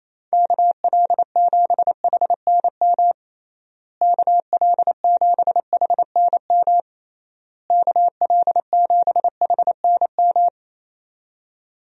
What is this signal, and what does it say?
Morse code, heard as a keyed tone, about 28 wpm, 700 Hz. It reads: KL7HNM KL7HNM KL7HNM
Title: USA Call Signs - 6 Characters - Repeated 3x - 28wpm